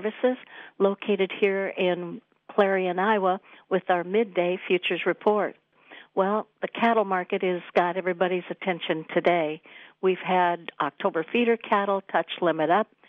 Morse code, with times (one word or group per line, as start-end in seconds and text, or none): none